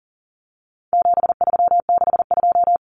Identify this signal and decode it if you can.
Morse code, heard as a keyed tone, about 40 wpm, 700 Hz. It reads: 7362
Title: Numbers - 4-Digit 40wpm